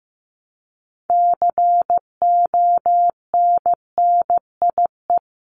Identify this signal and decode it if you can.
Morse code, heard as a keyed tone, about 15 wpm, 700 Hz. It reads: CONNIE
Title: CWT Exchange 15wpm